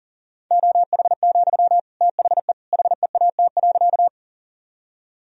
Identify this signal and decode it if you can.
Morse code, heard as a keyed tone, about 40 wpm, 700 Hz. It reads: OH, THE HEAT.
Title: Sentences from Top 1000 Words - Review Part 3 - 40wpm